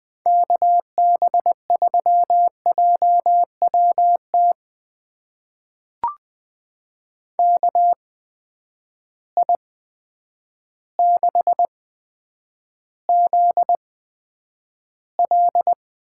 Morse code, heard as a keyed tone, about 20 wpm, spaced as for 9 wpm, 700 Hz. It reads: KB3JWT KI6ZL
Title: USA Call Signs - 6 Characters - 8x Character Spacing - 20wpm